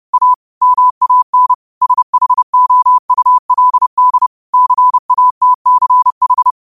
Morse code, heard as a keyed tone, about 30 wpm, 1000 Hz. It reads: A MAN SHOULD CATCH